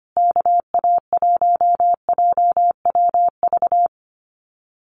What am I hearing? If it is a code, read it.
Morse code, heard as a keyed tone, about 25 wpm, 700 Hz. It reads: KA1JWV
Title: Common QSO Phrases Encore 25wpm